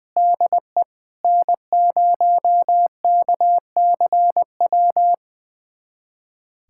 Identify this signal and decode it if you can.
Morse code, heard as a keyed tone, about 20 wpm, 700 Hz. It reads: DE N0KCW
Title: USA Callsign DE Callsign - 5 Characters - 20wpm